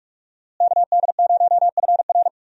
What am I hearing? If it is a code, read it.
Morse code, heard as a keyed tone, about 42 wpm, 700 Hz. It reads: KD0FR
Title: USA Call Signs - 5 Characters - Repeated 3x - 45wpm